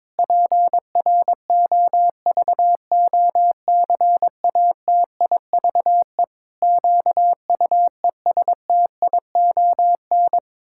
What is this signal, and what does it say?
Morse code, heard as a keyed tone, about 22 wpm, 700 Hz. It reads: PROVOCATIVE QUESTION
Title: Pangram Challenge v2 - 22wpm